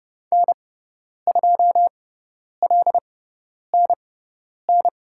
Morse code, heard as a keyed tone, about 31 wpm, spaced as for 10 wpm, 700 Hz. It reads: N2LNN
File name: Lesson 17g - Letter u - Call Signs 30wpm